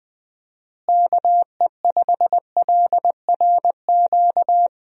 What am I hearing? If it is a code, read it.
Morse code, heard as a keyed tone, about 21 wpm, 700 Hz. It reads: KE5LRQ
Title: USA Call Signs - 6 Characters - Repeated 2x - 20wpm